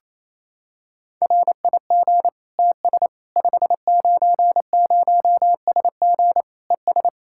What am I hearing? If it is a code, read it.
Morse code, heard as a keyed tone, about 28 wpm, 700 Hz. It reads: RIG TS 590SG ES
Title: Practice QSO - Exchange 2 - Info - 28wpm